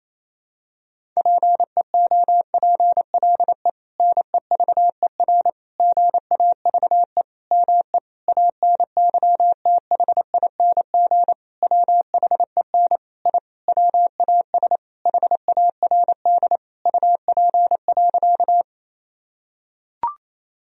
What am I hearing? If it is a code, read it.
Morse code, heard as a keyed tone, about 28 wpm, 700 Hz. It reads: PEOPLE NEVER GAVE ME ANYTHING WHEN I WAS HARD UP.
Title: Sentences from Top 1000 Words - Review Part 1 - 28wpm